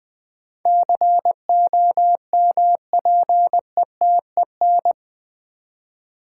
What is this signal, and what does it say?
Morse code, heard as a keyed tone, about 20 wpm, 700 Hz. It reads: COMPETEN
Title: Build A Word - 4 Letters At A Time - 10-Letter Words - 20wpm